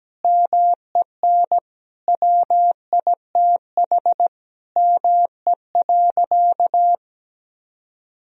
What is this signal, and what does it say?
Morse code, heard as a keyed tone, about 17 wpm, 700 Hz. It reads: MEN WITH ME.
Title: Sentences from Top 1000 Words - Review Part 2 - 17wpm